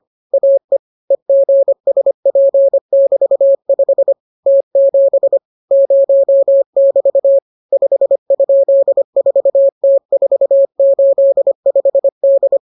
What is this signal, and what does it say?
Morse code, heard as a keyed tone, about 25 wpm, 550 Hz. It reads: AE EGSP<BT>5 T7 0<BT> 5?4T485D